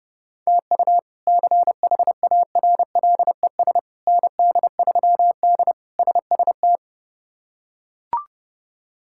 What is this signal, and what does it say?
Morse code, heard as a keyed tone, about 31 wpm, 700 Hz. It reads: TU CHARLES ND3D SST
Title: K1USN SST QSO Practice - 30wpm